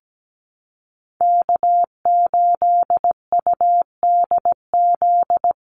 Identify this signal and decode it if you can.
Morse code, heard as a keyed tone, about 17 wpm, 700 Hz. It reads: K8UDZ
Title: USA Call Signs Repeated 3x 17wpm